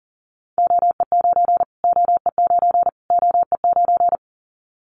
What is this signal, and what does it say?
Morse code, heard as a keyed tone, about 40 wpm, 700 Hz. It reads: OE9 OE9 OE9
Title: Callsign Prefixes for Top 25 DXCC - Repeated 3x - 40wpm